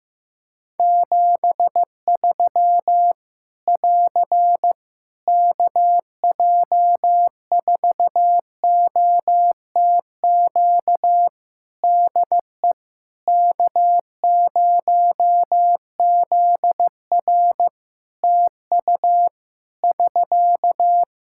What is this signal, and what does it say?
Morse code, heard as a keyed tone, about 15 wpm, 700 Hz. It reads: 73 <AR> KJ4OTQ DE K0ZR TU <SK>
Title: Practice QSO - Exchange 4 - Ending - 15wpm